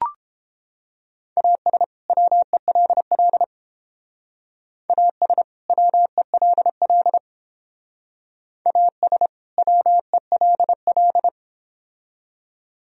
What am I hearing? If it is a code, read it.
Morse code, heard as a keyed tone, about 29 wpm, 700 Hz. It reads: AS WELL AS WELL AS WELL